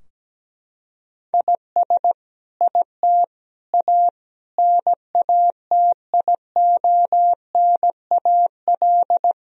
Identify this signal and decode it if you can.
Morse code, heard as a keyed tone, about 17 wpm, 700 Hz. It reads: IS IT A NATIONAL
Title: Sentences from Top 1000 Words - Review Part 4 - 17wpm